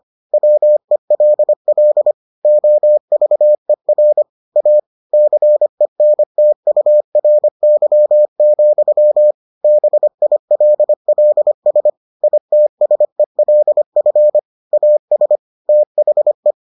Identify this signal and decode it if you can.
Morse code, heard as a keyed tone, about 25 wpm, 600 Hz. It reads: WELL OVER A CENTURY, BILLS ITSELF AS THE